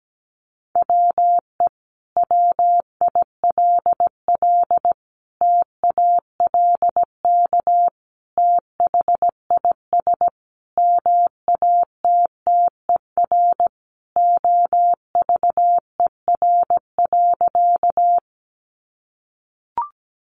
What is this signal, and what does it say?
Morse code, heard as a keyed tone, about 17 wpm, 700 Hz. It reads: WE WILL TALK THIS MATTER OVER.
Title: Sentences from Top 700 Words 17wpm